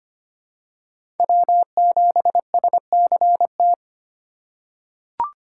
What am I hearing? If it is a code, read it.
Morse code, heard as a keyed tone, about 25 wpm, 700 Hz. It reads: W7SCT
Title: USA Call Signs - 5 Characters - 4x Character Spacing - 25wpm